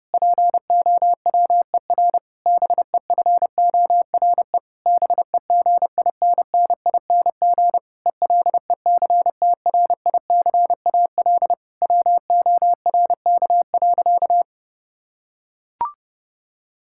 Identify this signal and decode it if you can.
Morse code, heard as a keyed tone, about 30 wpm, 700 Hz. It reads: POWER BEFORE BEGINNING ELECTRICAL WORK.
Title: Sentences from Top 1000 Words - Review Part 6 - 30wpm